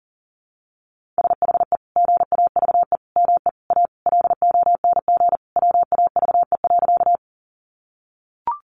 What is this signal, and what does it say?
Morse code, heard as a keyed tone, about 40 wpm, 700 Hz. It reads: SHE GAVE ME A LONG WAVE.
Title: Sentences from Top 1400 Words 40wpm